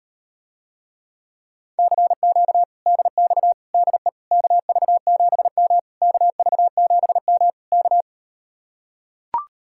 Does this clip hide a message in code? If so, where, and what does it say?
Yes: Morse code from 1.8 to 8.0 s, heard as a keyed tone, about 38 wpm, 700 Hz. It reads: CQ DX DE KV7M KV7M K